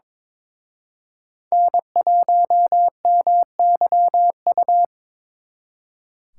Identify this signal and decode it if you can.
Morse code, heard as a keyed tone, about 26 wpm, 700 Hz. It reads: N1MYU
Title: Speed Racing - USA 1x3 Call Signs 22wpm